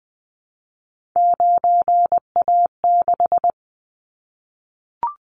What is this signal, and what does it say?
Morse code, heard as a keyed tone, about 20 wpm, 700 Hz. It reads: 9A6
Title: Callsign Prefixes for Top 100 DXCC - 20wpm